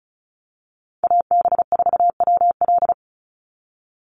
Morse code, heard as a keyed tone, about 34 wpm, 700 Hz. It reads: AB4WL
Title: NA SOTA Chaser Callsigns 35wpm